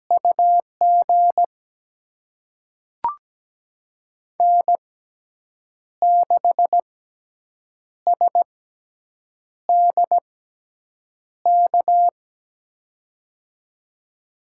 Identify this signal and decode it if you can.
Morse code, heard as a keyed tone, about 17 wpm, spaced as for 7 wpm, 700 Hz. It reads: UG N6SDK
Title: USA Call Signs - 5 Characters - 6x Character Spacing - 17wpm